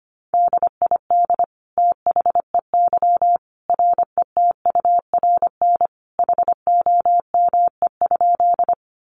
Morse code, heard as a keyed tone, about 25 wpm, 700 Hz. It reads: DID THEY RETURN HOME?
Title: Sentences from Top 800 Words 25wpm